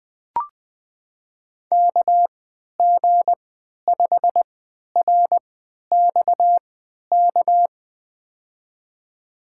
Morse code, heard as a keyed tone, about 20 wpm, spaced as for 11 wpm, 700 Hz. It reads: KG5RXK
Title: USA Call Signs - 3x Character Spacing - 20wpm